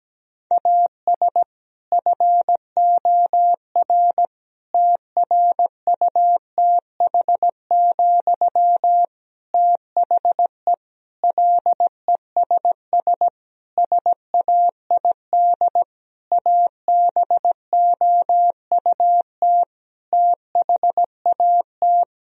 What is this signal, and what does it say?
Morse code, heard as a keyed tone, about 17 wpm, 700 Hz. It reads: AS FOR TRUTH, THE LESS SAID ABOUT THAT
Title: Sentences from Top 1000 Words - Review Part 2 - 17wpm